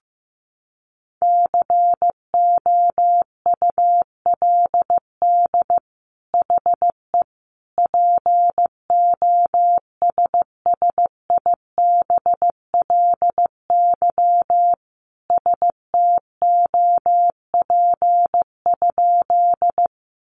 Morse code, heard as a keyed tone, about 15 wpm, 700 Hz. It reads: COULD HE POSSIBLY STOP?
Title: Sentences from Top 1300 Words 15wpm